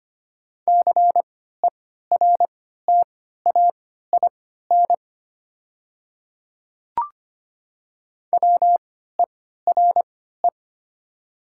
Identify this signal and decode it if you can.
Morse code, heard as a keyed tone, about 25 wpm, spaced as for 14 wpm, 690 Hz. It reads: CERTAIN WERE
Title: 3x Character Spacing - Top 500 Words - 25wpm